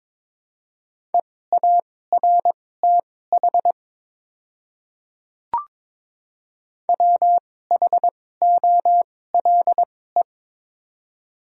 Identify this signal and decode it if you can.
Morse code, heard as a keyed tone, about 22 wpm, spaced as for 16 wpm, 700 Hz. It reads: EARTH WHOLE
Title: 2x Character Spacing - Top 500 Words - 22wpm